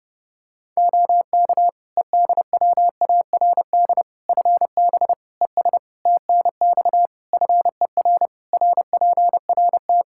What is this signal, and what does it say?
Morse code, heard as a keyed tone, about 30 wpm, 700 Hz. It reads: OK EDWARD FB ES TNX FER RPRT
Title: Practice QSO - Exchange 2 - Info - 30wpm